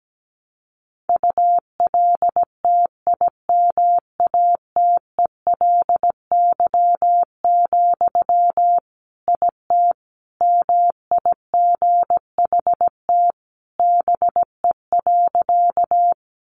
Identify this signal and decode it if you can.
Morse code, heard as a keyed tone, about 17 wpm, 700 Hz. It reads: ULTIMATELY, IT MIGHT BE.
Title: Sentences from Top 1200 Words 17wpm